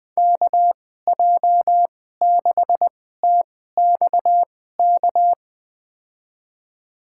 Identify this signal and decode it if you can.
Morse code, heard as a keyed tone, about 20 wpm, spaced as for 14 wpm, 700 Hz. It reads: KJ6TXK
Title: USA Call Signs - 6 Characters - 2x Character Spacing - 20wpm